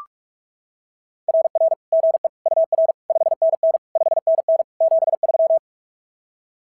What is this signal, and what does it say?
Morse code, heard as a keyed tone, about 44 wpm, 690 Hz. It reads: RR GE UR 5NN 5NN 73
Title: SOTA QSO Practice 45wpm